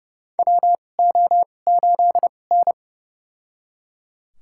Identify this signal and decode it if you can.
Morse code, heard as a keyed tone, about 31 wpm, spaced as for 22 wpm, 700 Hz. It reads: WO8N
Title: USA Call Signs - 4 Characters - 2x Character Spacing - 30wpm